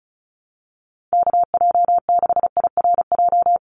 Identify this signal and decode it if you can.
Morse code, heard as a keyed tone, about 34 wpm, 700 Hz. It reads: KJ6IRJ
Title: USA Call Signs 35wpm